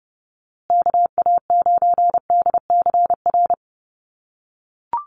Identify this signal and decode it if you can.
Morse code, heard as a keyed tone, about 30 wpm, 700 Hz. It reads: KA9DCR
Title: USA Call Signs Repeated 2x 30wpm